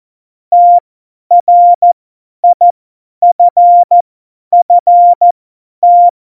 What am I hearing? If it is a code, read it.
Morse code, heard as a keyed tone, about 14 wpm, spaced as for 10 wpm, 700 Hz. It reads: TRIFFT
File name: Morsekurs / Witze / Witz 15. Dezember 2020, 50 F70